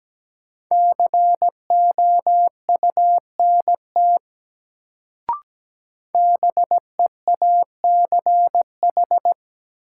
Morse code, heard as a keyed tone, about 17 wpm, 700 Hz. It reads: COUNT BEACH
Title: Sentences from Top 1500 Words - New Word Drill - 17wpm